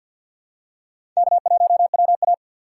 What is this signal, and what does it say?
Morse code, heard as a keyed tone, about 49 wpm, 690 Hz. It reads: K1WA
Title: USA Call Signs - 4x Character Spacing - 50wpm